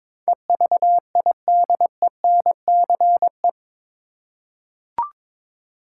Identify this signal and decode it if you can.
Morse code, heard as a keyed tone, about 22 wpm, 700 Hz. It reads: EVIDENCE
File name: practice / 22wpm / Eight Letter Words - 22wpm